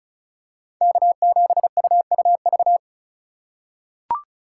Morse code, heard as a keyed tone, about 36 wpm, 700 Hz. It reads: K7UUV
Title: USA Call Signs - 5 Characters - 2x Character Spacing - 35wpm